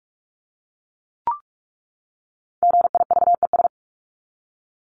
Morse code, heard as a keyed tone, about 44 wpm, 700 Hz. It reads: GIVES
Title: ES:S Suffix - Top 100 Words with Base Words 45wpm